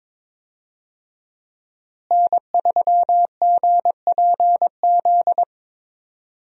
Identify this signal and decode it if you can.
Morse code, heard as a keyed tone, about 22 wpm, 700 Hz. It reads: N3GPZ
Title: USA Call Signs - 5 Characters - Repeated 2x - 22wpm